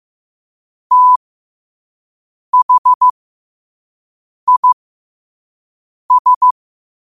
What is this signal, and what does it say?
Morse code, heard as a keyed tone, about 15 wpm, spaced as for 5 wpm, 1000 Hz. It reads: THIS
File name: qotdfiles / 5wpm